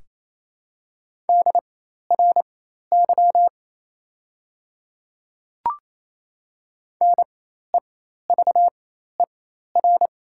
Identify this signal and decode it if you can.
Morse code, heard as a keyed tone, about 28 wpm, spaced as for 13 wpm, 700 Hz. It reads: DRY NEVER
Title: 4x Character Spacing - Top 500 Words - 28wpm